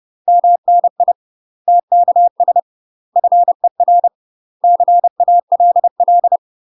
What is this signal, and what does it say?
Morse code, heard as a keyed tone, about 30 wpm, 700 Hz. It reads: MNI TKS FER CALL
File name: UK MASTER QSOs TRAINING / 30WPM / UK MASTER QSOs 0026